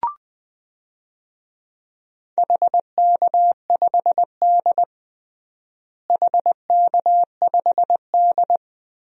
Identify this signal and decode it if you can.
Morse code, heard as a keyed tone, about 21 wpm, 700 Hz. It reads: HK5D HK5D
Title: DXCC-100 Call Signs - 4 Characters - Repeated 2x - 20wpm